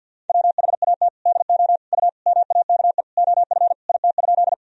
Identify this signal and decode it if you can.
Morse code, heard as a keyed tone, about 48 wpm, 700 Hz. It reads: WHAT DO U MAKE OF IT?
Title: Sentences from Top 100 Words and Abbreviations 50wpm